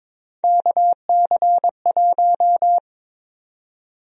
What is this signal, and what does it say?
Morse code, heard as a keyed tone, about 23 wpm, 700 Hz. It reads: KC1
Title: USA Call Sign Prefixes - 22wpm